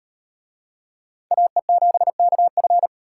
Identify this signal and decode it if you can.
Morse code, heard as a keyed tone, about 38 wpm, 700 Hz. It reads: AE7KF